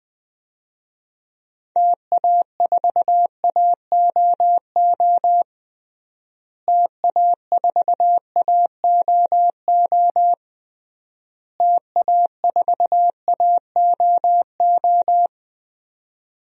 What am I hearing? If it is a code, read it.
Morse code, heard as a keyed tone, about 21 wpm, 700 Hz. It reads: TA4AOO TA4AOO TA4AOO
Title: DXCC-100 Call Signs - 6 Characters - Repeated 3x - 20wpm